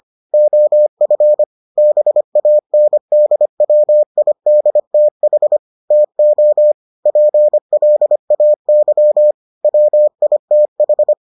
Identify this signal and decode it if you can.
Morse code, heard as a keyed tone, about 25 wpm, 600 Hz. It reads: OF BANDWIDTH TO PLAY WITH